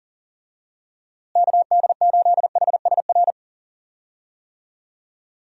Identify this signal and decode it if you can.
Morse code, heard as a keyed tone, about 37 wpm, 700 Hz. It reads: KD8HSR
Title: USA Call Signs - 6 Characters - Repeated 2x - 40wpm